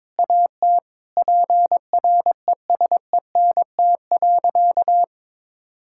Morse code, heard as a keyed tone, about 21 wpm, 700 Hz. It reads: AT PRESENT.
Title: Sentences from Top 1000 Words - Review Part 5 - 22wpm